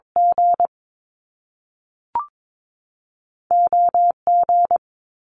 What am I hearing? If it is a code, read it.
Morse code, heard as a keyed tone, about 22 wpm, spaced as for 3 wpm, 700 Hz. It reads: GO G
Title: Top Two Letter Combinations in Words 22wpm